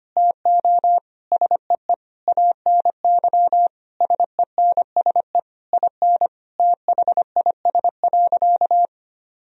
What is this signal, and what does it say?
Morse code, heard as a keyed tone, about 25 wpm, 700 Hz. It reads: TO SEE ANY SENSE IN THIS.
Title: Sentences from Top 1000 Words - Review Part 4 - 25wpm